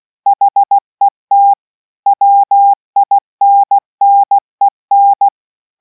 Morse code, heard as a keyed tone, about 16 wpm, 800 Hz. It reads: HET WINNEN